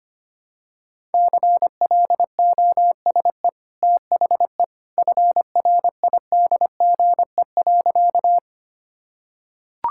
Morse code, heard as a keyed tone, about 25 wpm, 690 Hz. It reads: CLOSE THE FRIDGE.
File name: Easy Sentences with Hints - 25wpm